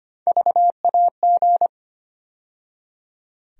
Morse code, heard as a keyed tone, about 25 wpm, 700 Hz. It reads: VAG
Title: USA Call Sign Postfixes - 3 Letters 25wpm